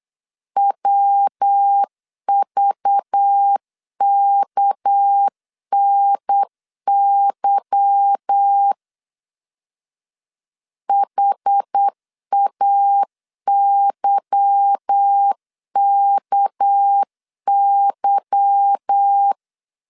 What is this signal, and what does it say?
Morse code, heard as a keyed tone, about 8 wpm, 800 Hz. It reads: WVKNY HAYKY
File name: LEZIONI / 8 YJ45 / 8 Y 8wpm 3245